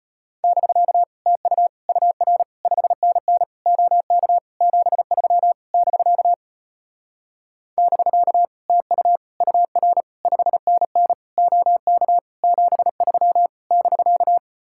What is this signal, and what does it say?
Morse code, heard as a keyed tone, about 33 wpm, 700 Hz. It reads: <BK> TU UR 5NN OK 73 <BK> <BK> TU UR 5NN OK 73 <BK>